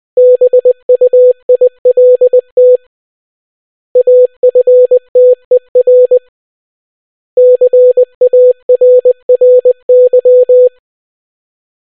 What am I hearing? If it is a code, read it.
Morse code, heard as a keyed tone, about 20 wpm, 500 Hz. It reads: BUILT AFTER CARRY